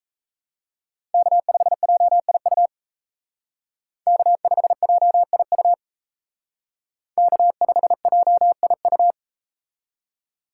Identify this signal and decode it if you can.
Morse code, heard as a keyed tone, about 39 wpm, 700 Hz. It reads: K5JIU K5JIU K5JIU